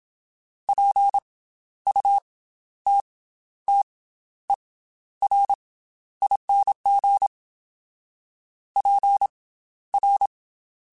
Morse code, heard as a keyed tone, about 27 wpm, spaced as for 12 wpm, 800 Hz. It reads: PUTTERING PR
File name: Special INGs / Special INGs-mp3s / Slow ing @25